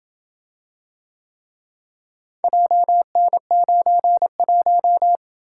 Morse code, heard as a keyed tone, about 26 wpm, 700 Hz. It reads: JN91